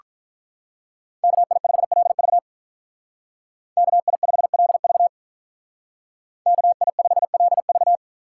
Morse code, heard as a keyed tone, about 47 wpm, 700 Hz. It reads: KI5LV KI5LV KI5LV